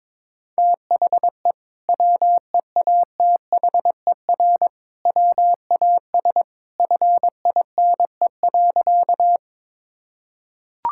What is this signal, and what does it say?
Morse code, heard as a keyed tone, about 22 wpm, 700 Hz. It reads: THE WEATHER WAS FINE.
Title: Sentences from Top 1000 Words - Review Part 5 - 22wpm